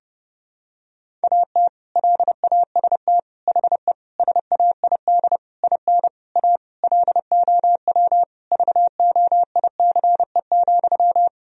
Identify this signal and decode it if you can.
Morse code, heard as a keyed tone, about 30 wpm, 700 Hz. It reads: AT LAST HE SAID IN A LOW VOICE,
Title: Sentences from Top 1000 Words - Review Part 1 - 30wpm